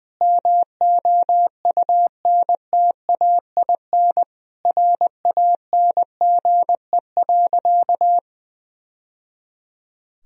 Morse code, heard as a keyed tone, about 20 wpm, 700 Hz. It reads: MOUNTAIN RANGE.
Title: Sentences from Top 1000 Words - Review Part 4 - 20wpm